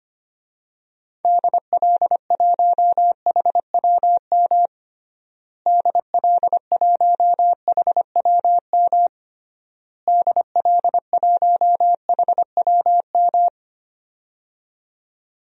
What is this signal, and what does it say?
Morse code, heard as a keyed tone, about 25 wpm, 700 Hz. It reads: DL1HWM DL1HWM DL1HWM
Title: DXCC-100 Call Signs - 6 Characters - Repeated 3x - 25wpm